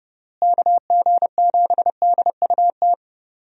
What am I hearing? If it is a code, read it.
Morse code, heard as a keyed tone, about 32 wpm, 700 Hz. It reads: KG7DUT
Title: USA Call Signs - 6 Characters - Repeated 3x - 30wpm